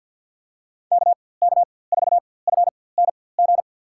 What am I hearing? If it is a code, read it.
Morse code, heard as a keyed tone, about 53 wpm, spaced as for 24 wpm, 700 Hz. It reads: KK4FNG
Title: USA Call Signs - 4x Character Spacing - 50wpm